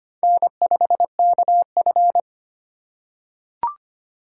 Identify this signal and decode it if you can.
Morse code, heard as a keyed tone, about 25 wpm, 700 Hz. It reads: N5KF
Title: USA Call Signs - 4x Character Spacing - 25wpm